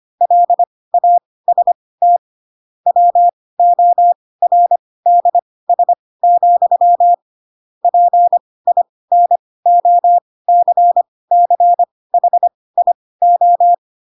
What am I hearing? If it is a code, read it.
Morse code, heard as a keyed tone, about 25 wpm, spaced as for 18 wpm, 700 Hz. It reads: LAST WORDS, PINOCCHIO
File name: Pinocchio / Pinocchio ch4 12 25 ik1hge